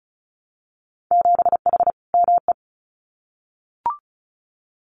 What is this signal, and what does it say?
Morse code, heard as a keyed tone, about 34 wpm, 700 Hz. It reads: 7H ME
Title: Winter Field Day Exchanges Encore 35wpm